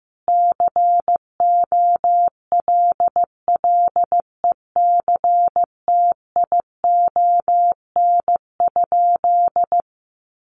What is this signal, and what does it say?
Morse code, heard as a keyed tone, about 14 wpm, 700 Hz. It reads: COLLECTION?
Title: Sentences from Top 1200 Words 15wpm